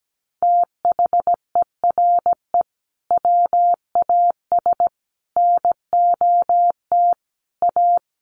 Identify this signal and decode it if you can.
Morse code, heard as a keyed tone, about 17 wpm, 700 Hz. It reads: THERE WAS NOT A
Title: Sentences from Top 1100 Words 17wpm